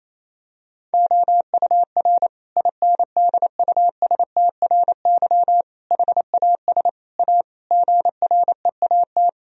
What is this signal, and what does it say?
Morse code, heard as a keyed tone, about 28 wpm, 700 Hz. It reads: OUR INDUSTRY HAS A GREAT
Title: Sentences from Top 1000 Words - Review Part 4 - 28wpm